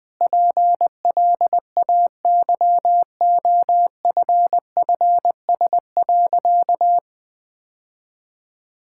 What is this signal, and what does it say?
Morse code, heard as a keyed tone, about 20 wpm, 700 Hz. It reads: PLAYOFFS.
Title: Sentences with 7-Word Phrases - Spoken Beginning - 20wpm